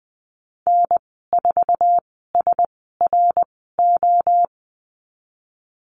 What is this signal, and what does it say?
Morse code, heard as a keyed tone, about 20 wpm, spaced as for 14 wpm, 700 Hz. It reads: N4SRO
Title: 055 Lesson 10g - Letter r - Call Signs 20wpm